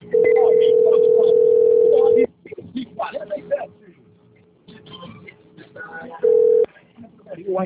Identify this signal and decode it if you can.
Ringback tone (North America), 440 + 480 Hz, with speech.
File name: out- - - - - .78152